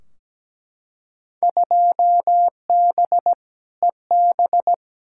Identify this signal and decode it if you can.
Morse code, heard as a keyed tone, about 17 wpm, 700 Hz. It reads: 2B EB
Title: Field Day Exchanges - 17wpm